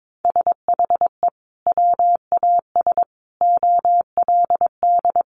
Morse code, heard as a keyed tone, about 22 wpm, 700 Hz. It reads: SHE WAS OLD
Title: Sentences from Top 800 Words 22wpm